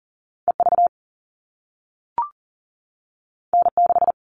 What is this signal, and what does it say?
Morse code, heard as a keyed tone, about 40 wpm, 700 Hz. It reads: EV N6
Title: All Cut Numbers - 2-Digit 40wpm